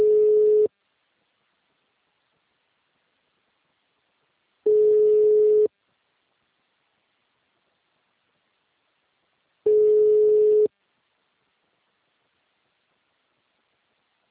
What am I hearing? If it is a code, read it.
Ringback tone (Europe, 425 Hz).